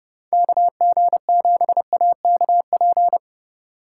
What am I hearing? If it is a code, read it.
Morse code, heard as a keyed tone, about 30 wpm, 700 Hz. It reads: KG7AKP
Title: USA Call Signs - 6 Characters - 3x Character Spacing - 30wpm